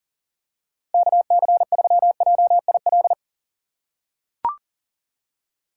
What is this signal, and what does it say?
Morse code, heard as a keyed tone, about 38 wpm, 700 Hz. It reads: KC3JIL
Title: USA Call Signs - 6 Characters - 3x Character Spacing - 40wpm